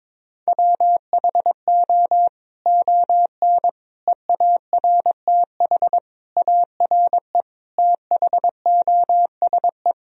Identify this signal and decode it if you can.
Morse code, heard as a keyed tone, about 21 wpm, 700 Hz. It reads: WHO ON EARTH ARE THOSE
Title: Sentences from Top 1000 Words - Review Part 1 - 22wpm